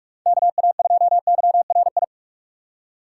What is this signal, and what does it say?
Morse code, heard as a keyed tone, about 47 wpm, 700 Hz. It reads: KA2YRI
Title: USA Call Signs - 2x Character Spacing - 45wpm